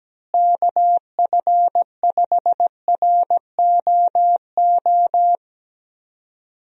Morse code, heard as a keyed tone, about 17 wpm, 700 Hz. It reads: KF5ROO
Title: USA Call Signs - 6 Characters - Repeated 3x - 17wpm